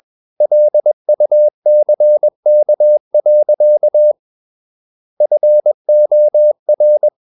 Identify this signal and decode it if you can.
Morse code, heard as a keyed tone, about 21 wpm, 600 Hz. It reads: LUCK. FOR
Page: CW med Gnister